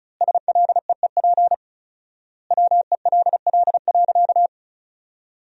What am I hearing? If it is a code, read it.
Morse code, heard as a keyed tone, about 36 wpm, 700 Hz. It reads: SLEEP WELL.
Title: 4x Word-Spacing - Sentences from Top 500 Words - 35wpm